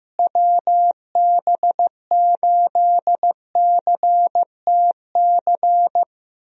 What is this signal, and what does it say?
Morse code, heard as a keyed tone, about 15 wpm, 700 Hz. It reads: WB8CTC
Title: USA Call Signs - 6 Characters - Repeated 3x - 15wpm